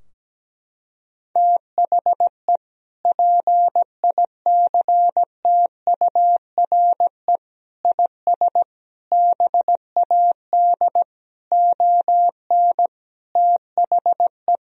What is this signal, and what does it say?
Morse code, heard as a keyed tone, about 17 wpm, 700 Hz. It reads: THE PICTURE IS BAD ON THE
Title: Sentences from Top 1000 Words - Review Part 4 - 17wpm